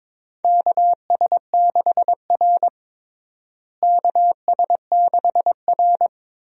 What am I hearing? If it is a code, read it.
Morse code, heard as a keyed tone, about 22 wpm, 700 Hz. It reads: KS6R KS6R
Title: USA Call Signs - 4 Characters - Repeated 2x - 22wpm